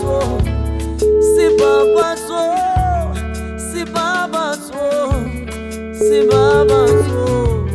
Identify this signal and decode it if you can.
Ringback tone (Europe, 425 Hz), with noise.